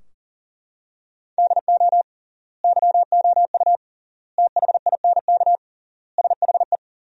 Morse code, heard as a keyed tone, about 40 wpm, 700 Hz. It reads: DO YOU THINK SHE
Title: 3x Word-Spacing - Sentences from Top 200 Words - 40wpm